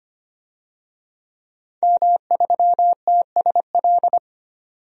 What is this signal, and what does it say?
Morse code, heard as a keyed tone, about 25 wpm, 700 Hz. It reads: M3TSL
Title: DXCC-100 Call Signs - 5 Characters - 25wpm